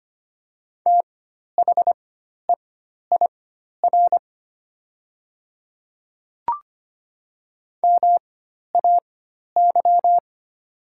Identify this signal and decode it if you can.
Morse code, heard as a keyed tone, about 25 wpm, spaced as for 12 wpm, 700 Hz. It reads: THEIR MAY
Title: 4x Character Spacing - Top 100 Words - 25wpm